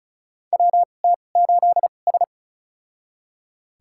Morse code, heard as a keyed tone, about 36 wpm, spaced as for 25 wpm, 700 Hz. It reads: WT8S
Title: USA Call Signs - 4 Characters - 2x Character Spacing - 35wpm